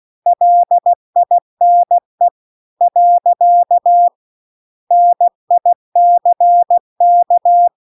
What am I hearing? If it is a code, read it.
Morse code, heard as a keyed tone, about 16 wpm, 700 Hz. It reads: LINE . NICK
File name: CW-The-Big-Twoheaded-River-2-16WPM